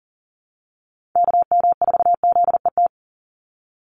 Morse code, heard as a keyed tone, about 40 wpm, 700 Hz. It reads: KM4ZET
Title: USA Call Signs 40wpm